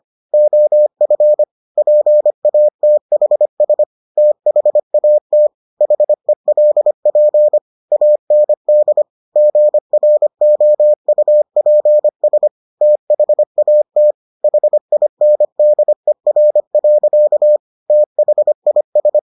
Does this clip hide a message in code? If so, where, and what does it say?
Yes: Morse code from 0.3 to 19.2 s, heard as a keyed tone, about 25 wpm, 600 Hz. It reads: OF PATHS THAT HELP AND GROUPS THAT HINDER. THIS